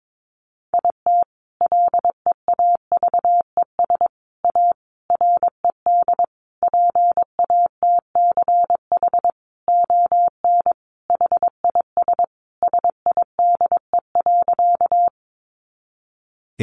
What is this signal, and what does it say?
Morse code, heard as a keyed tone, about 22 wpm, 700 Hz. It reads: IT LEAVES A RED PATCH ON HIS SIDE.
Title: A Sound Of Thunder 22wpm